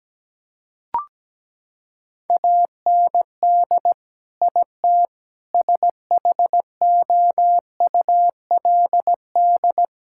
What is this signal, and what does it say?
Morse code, heard as a keyed tone, about 17 wpm, 700 Hz. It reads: AND IT SHOULD
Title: Sentences from Top 1000 Words - Review Part 6 - 17wpm